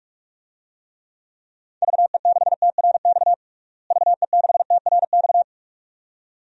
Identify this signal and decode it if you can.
Morse code, heard as a keyed tone, about 46 wpm, 700 Hz. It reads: VE6TRX VE6TRX
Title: DXCC-100 Call Signs - 6 Characters - Repeated 2x - 45wpm